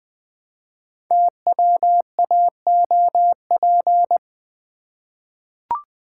Morse code, heard as a keyed tone, about 20 wpm, 700 Hz. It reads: TWAOP
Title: Code Groups - 5 Letters - 20wpm